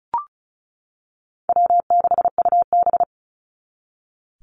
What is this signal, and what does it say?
Morse code, heard as a keyed tone, about 34 wpm, 700 Hz. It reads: W6UB
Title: NA SOTA Activator Callsigns 35wpm